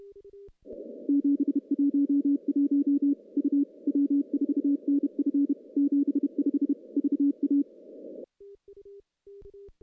Morse code, heard as a keyed tone, about 31 wpm, 280 Hz. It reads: 711 U W4NF 75 VA
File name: contests / CW-SS-k5tr- -1400